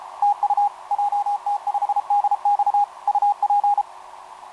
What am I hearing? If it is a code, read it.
Morse code, heard as a keyed tone, about 31 wpm, 820 Hz, with radio noise. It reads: TU JT5DX UP
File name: jt5dx f5idj etc 12m cw 18mar2011 0915